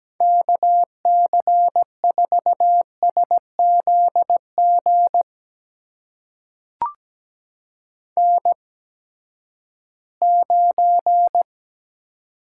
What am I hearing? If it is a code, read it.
Morse code, heard as a keyed tone, about 17 wpm, spaced as for 12 wpm, 700 Hz. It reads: KC4SZG N9
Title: USA Call Signs - 8x Character Spacing - 17wpm